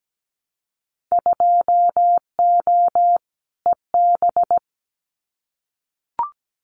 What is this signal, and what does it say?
Morse code, heard as a keyed tone, about 17 wpm, 700 Hz. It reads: 2O EB
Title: Winter Field Day Exchanges Encore 17wpm